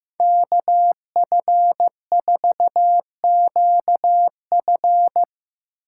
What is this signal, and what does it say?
Morse code, heard as a keyed tone, about 16 wpm, 700 Hz. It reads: KF4QF
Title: Speed Racing - USA 2x2 Call Signs 15wpm